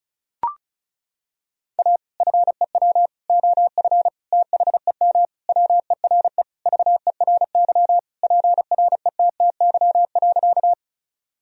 Morse code, heard as a keyed tone, about 36 wpm, 700 Hz. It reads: A FEW OF THEM WERE VERY PRETTY.
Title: Sentences from Top 1000 Words - Review Part 1 - 35wpm